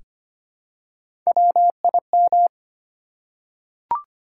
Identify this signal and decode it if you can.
Morse code, heard as a keyed tone, about 25 wpm, 700 Hz. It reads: WIM
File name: Three Letter-Number Combinations - 25wpm